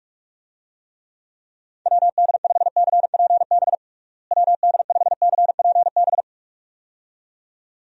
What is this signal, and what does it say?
Morse code, heard as a keyed tone, about 43 wpm, 700 Hz. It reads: WD5CPB WD5CPB
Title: USA Call Signs - 6 Characters - Repeated 2x - 45wpm